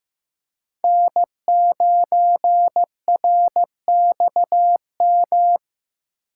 Morse code, heard as a keyed tone, about 18 wpm, 700 Hz. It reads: N9RXM
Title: Speed Racing - USA 1x3 Call Signs 15wpm